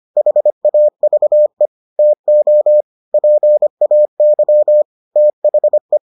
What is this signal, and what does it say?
Morse code, heard as a keyed tone, about 25 wpm, 600 Hz. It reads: HAVE TO PAY THE